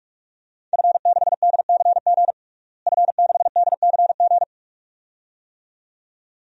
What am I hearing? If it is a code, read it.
Morse code, heard as a keyed tone, about 46 wpm, 700 Hz. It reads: F6DCG F6DCG
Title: DXCC-100 Call Signs - 5 Characters - Repeated 2x - 45wpm